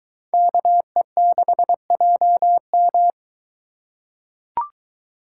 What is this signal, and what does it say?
Morse code, heard as a keyed tone, about 22 wpm, 700 Hz. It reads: KE6JM